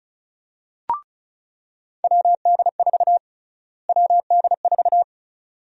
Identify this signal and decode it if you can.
Morse code, heard as a keyed tone, about 36 wpm, 700 Hz. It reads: WD4 WD4
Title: USA Call Sign Prefixes - Repeated 2x - 35wpm